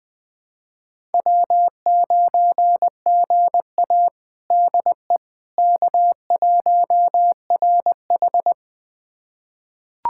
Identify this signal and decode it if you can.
Morse code, heard as a keyed tone, about 20 wpm, 700 Hz. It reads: W9GA DE K1RH
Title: USA Callsign DE Callsign - 4 Characters - 20wpm